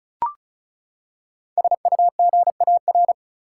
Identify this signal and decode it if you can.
Morse code, heard as a keyed tone, about 34 wpm, 700 Hz. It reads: SUGAR
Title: Five Letter Words - 35wpm